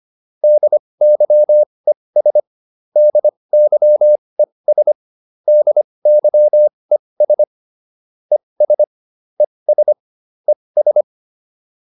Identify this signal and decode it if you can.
Morse code, heard as a keyed tone, about 25 wpm, spaced as for 20 wpm, 600 Hz. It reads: DYES DYES DYES ES ES ES